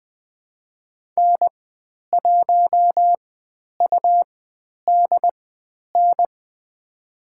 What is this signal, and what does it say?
Morse code, heard as a keyed tone, about 20 wpm, spaced as for 10 wpm, 700 Hz. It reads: N1UDN
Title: 109 Lesson 18g - Letter m - Call Signs 20wpm